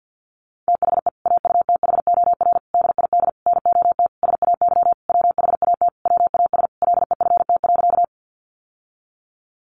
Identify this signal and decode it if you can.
Morse code, heard as a keyed tone, about 50 wpm, 700 Hz. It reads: THE AUTHOR DID NOT SAY WHAT WAS LEFT.
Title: Sentences from Top 1200 Words 50wpm